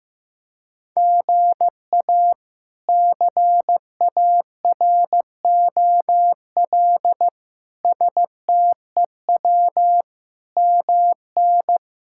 Morse code, heard as a keyed tone, about 15 wpm, 700 Hz. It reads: GA CAROL STEW MN
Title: K1USN SST QSO Practice - 15wpm